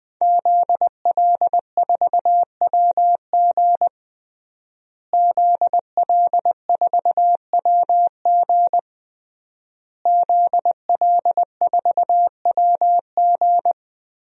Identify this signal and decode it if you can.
Morse code, heard as a keyed tone, about 20 wpm, 700 Hz. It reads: ZL4WG ZL4WG ZL4WG
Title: DXCC-100 Call Signs - 5 Characters - Repeated 3x - 20wpm